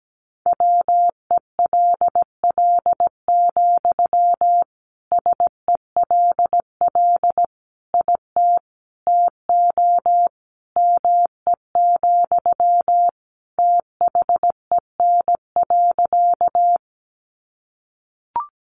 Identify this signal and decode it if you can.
Morse code, heard as a keyed tone, about 17 wpm, 700 Hz. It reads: WELL, SELL IT TO ME, THEN.
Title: Sentences from Top 1100 Words 17wpm